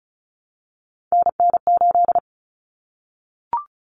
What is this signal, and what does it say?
Morse code, heard as a keyed tone, about 34 wpm, 700 Hz. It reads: NN8
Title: Common Cut Numbers - 3-Digit 35wpm